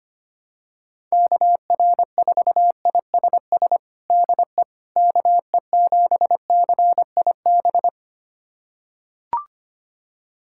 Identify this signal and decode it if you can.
Morse code, heard as a keyed tone, about 25 wpm, 700 Hz. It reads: KR4ISS DE KE7CIB
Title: USA Callsign DE Callsign - 6 Characters - 25wpm